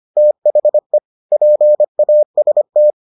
Morse code, heard as a keyed tone, about 25 wpm, 600 Hz. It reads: THE PAST